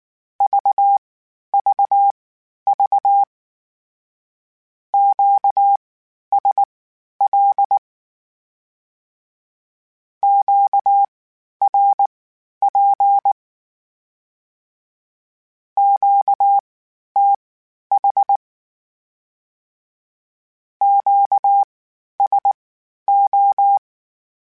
Morse code, heard as a keyed tone, about 19 wpm, spaced as for 11 wpm, 800 Hz. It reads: VVV QSL QRP QTH QSO
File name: (spaced) / Q-Calls Spaced-mp3s / Q Calls sp Once @18